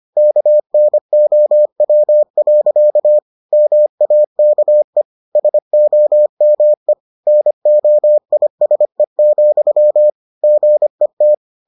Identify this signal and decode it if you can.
Morse code, heard as a keyed tone, about 25 wpm, 600 Hz. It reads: KNOW. MAKE SOME NOISE, GET